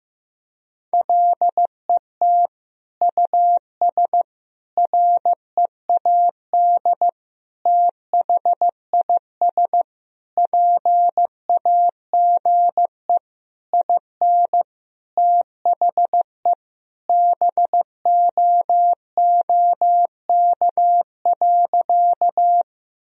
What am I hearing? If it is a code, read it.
Morse code, heard as a keyed tone, about 16 wpm, 700 Hz. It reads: LET US READ THIS PAGE IN THE BOOK.
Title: Sentences from Top 1000 Words - Review Part 1 - 15wpm